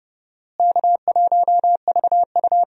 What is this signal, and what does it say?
Morse code, heard as a keyed tone, about 31 wpm, 700 Hz. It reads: K1VU
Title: USA Call Signs - 4 Characters - Repeated 3x - 30wpm